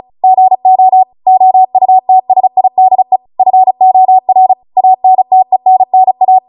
Morse code, heard as a keyed tone, about 36 wpm, 750 Hz. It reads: GO OUTSIDE FOR ANTENNA
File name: Morse / 35WPM